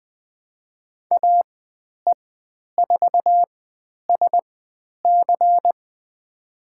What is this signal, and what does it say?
Morse code, heard as a keyed tone, about 20 wpm, spaced as for 10 wpm, 700 Hz. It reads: AE4SC
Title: 102 Lesson 17g - Letter u - Call Signs 20wpm